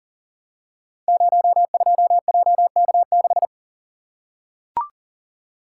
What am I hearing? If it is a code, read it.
Morse code, heard as a keyed tone, about 39 wpm, 700 Hz. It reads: 02JK6
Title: Code Groups - 5 Letter:Numbers - 40wpm